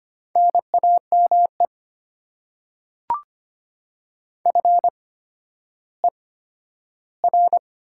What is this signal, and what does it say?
Morse code, heard as a keyed tone, about 25 wpm, spaced as for 12 wpm, 700 Hz. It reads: NAME FER
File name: Top 90 QSO Elements - 8x Character Spacing - 25wpm